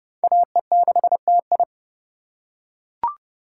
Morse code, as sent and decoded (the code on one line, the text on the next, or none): .- . -.... - ..
AE6TI